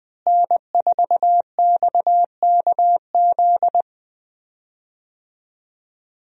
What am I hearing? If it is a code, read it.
Morse code, heard as a keyed tone, about 20 wpm, 700 Hz. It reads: N4XKZ
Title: USA Call Signs - 5 Characters - Repeated 2x - 20wpm